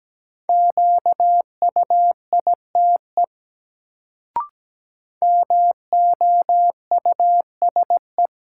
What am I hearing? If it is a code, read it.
Morse code, heard as a keyed tone, about 17 wpm, 700 Hz. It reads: QUITE MOUSE
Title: Sentences from Top 700 Words - New Word Drill - 17wpm